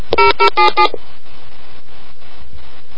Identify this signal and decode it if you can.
DTMF, heard as telephone touch tones, tones of about 110 ms. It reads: 0077